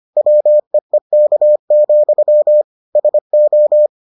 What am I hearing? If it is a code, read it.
Morse code, heard as a keyed tone, about 25 wpm, 600 Hz. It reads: WEEK, SO